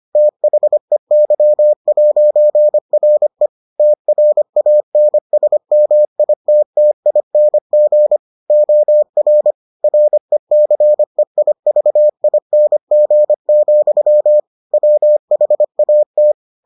Morse code, heard as a keyed tone, about 25 wpm, 600 Hz. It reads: THEY'RE TRANSMITTING OR RECEIVING, WHAT